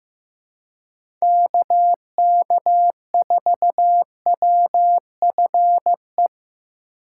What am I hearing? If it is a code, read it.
Morse code, heard as a keyed tone, about 15 wpm, 700 Hz. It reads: KK4WFE
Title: USA Call Signs - 6 Characters - 6x Character Spacing - 15wpm